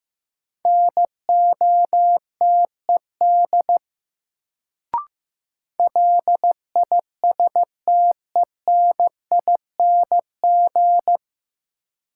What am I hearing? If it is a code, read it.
Morse code, heard as a keyed tone, about 15 wpm, 700 Hz. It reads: NOTED LISTENING
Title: Sets of 5 Words - New Word Drill - 15wpm